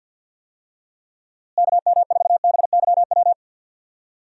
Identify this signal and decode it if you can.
Morse code, heard as a keyed tone, about 52 wpm, 700 Hz. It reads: KM4BYW
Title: USA Call Signs - 6 Characters - Repeated 3x - 50wpm